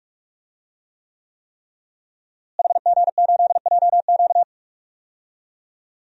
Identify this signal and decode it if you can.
Morse code, heard as a keyed tone, about 46 wpm, 700 Hz. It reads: HG8JQ
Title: DXCC-100 Call Signs - 5 Characters - Repeated 3x - 45wpm